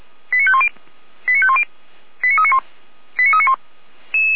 Five-tone selective call (ZVEI), heard as quick runs of five tones, tones of about 70 ms. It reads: 89510 89510 89391 89391